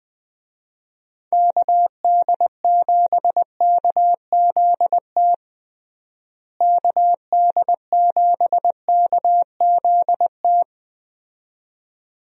Morse code, heard as a keyed tone, about 21 wpm, 700 Hz. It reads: KD7KZT KD7KZT
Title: USA Call Signs - 6 Characters - Repeated 2x - 20wpm